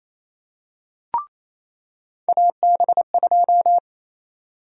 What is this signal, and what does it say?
Morse code, heard as a keyed tone, about 28 wpm, 700 Hz. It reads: AB2
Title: USA Call Sign Prefixes - 28wpm